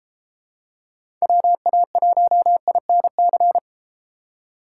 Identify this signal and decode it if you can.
Morse code, heard as a keyed tone, about 31 wpm, 700 Hz. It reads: WA1INC